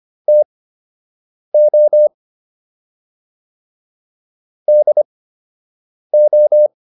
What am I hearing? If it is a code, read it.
Morse code, heard as a keyed tone, about 25 wpm, spaced as for 7 wpm, 600 Hz. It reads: TO DO